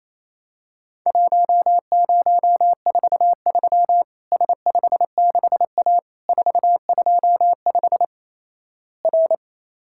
Morse code, heard as a keyed tone, about 28 wpm, 700 Hz. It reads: 1043 S56A 425 R
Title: Worked All Europe DX Contest - QTC Exchange - 28wpm